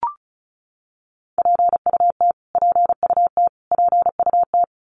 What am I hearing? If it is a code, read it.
Morse code, heard as a keyed tone, about 34 wpm, 700 Hz. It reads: PUT PUT PUT
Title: Top 200 Words Repeated 35wpm